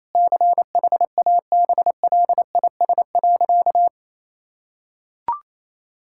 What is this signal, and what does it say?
Morse code, heard as a keyed tone, about 28 wpm, 700 Hz. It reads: CHABLIS.
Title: Pangram Challenge v2 - 28wpm